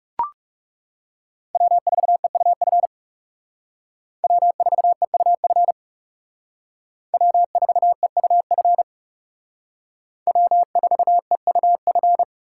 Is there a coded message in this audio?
Yes: Morse code from 1.6 to 12.2 s, heard as a keyed tone, about 38 wpm, 700 Hz. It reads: W4EUF W4EUF W4EUF W4EUF